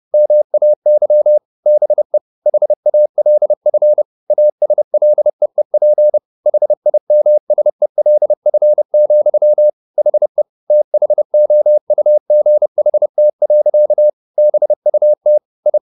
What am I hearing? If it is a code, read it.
Morse code, heard as a keyed tone, about 30 wpm, 600 Hz. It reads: MAY BE HALF ASLEEP HIMSELF, HE THOUGHT. BUT I